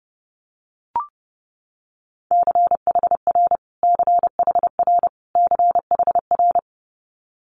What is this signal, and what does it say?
Morse code, heard as a keyed tone, about 30 wpm, 700 Hz. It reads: CHR CHR CHR
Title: Multi-letter Phonemes Repeated 3x 30wpm